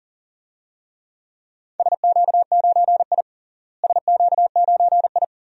Morse code, heard as a keyed tone, about 40 wpm, 700 Hz. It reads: SQ9I SQ9I
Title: DXCC-100 Call Signs - 4 Characters - Repeated 2x - 40wpm